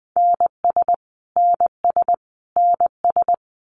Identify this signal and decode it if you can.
Morse code, heard as a keyed tone, about 20 wpm, 700 Hz. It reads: NS NS NS
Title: Canadian Provinces and Territories - Repeated 3x - 20wpm